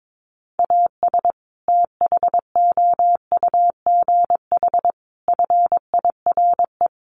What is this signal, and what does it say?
Morse code, heard as a keyed tone, about 22 wpm, 700 Hz. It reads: AS THOUGH FIRE
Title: Sentences from Top 1500 Words 22wpm